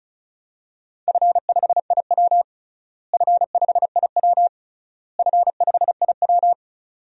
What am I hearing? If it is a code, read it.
Morse code, heard as a keyed tone, about 34 wpm, 700 Hz. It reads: F5IW F5IW F5IW
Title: DXCC-100 Call Signs - 4 Characters - Repeated 3x - 35wpm